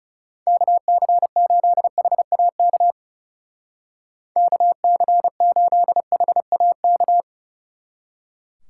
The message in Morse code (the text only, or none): KC8HAK KC8HAK